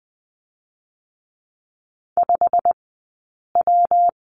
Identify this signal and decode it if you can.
Morse code, heard as a keyed tone, about 20 wpm, 700 Hz. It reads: 5 W
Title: 120 Lesson 20d - Number 3 - Two Characters 20wpm